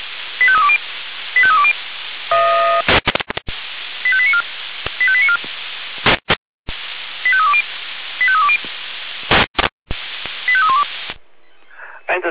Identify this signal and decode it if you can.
Five-tone selective call (ZVEI), heard as quick runs of five tones, tones of about 70 ms, with speech. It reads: 85320 85320 85894 85894 85320 85320 85312